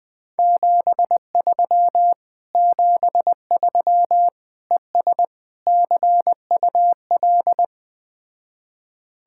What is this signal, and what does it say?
Morse code, heard as a keyed tone, about 20 wpm, 700 Hz. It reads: 73 73 ES CUL